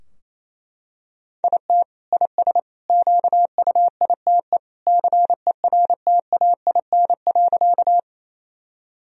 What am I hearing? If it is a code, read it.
Morse code, heard as a keyed tone, about 28 wpm, 700 Hz. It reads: IT IS QUITE CERTAIN.
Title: Sentences from Top 1000 Words - Review Part 2 - 28wpm